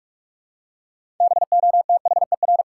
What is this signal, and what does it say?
Morse code, heard as a keyed tone, about 46 wpm, 700 Hz. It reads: BOTHER
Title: Six Letter Words - 45wpm